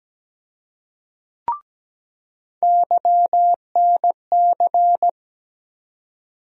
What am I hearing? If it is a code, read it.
Morse code, heard as a keyed tone, about 17 wpm, 700 Hz. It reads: YNC
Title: USA Call Sign Postfixes - 3 Letters 17wpm